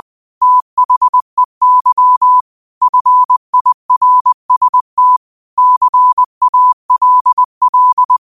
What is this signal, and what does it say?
Morse code, heard as a keyed tone, about 20 wpm, 1000 Hz. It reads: THEY FIRST CALL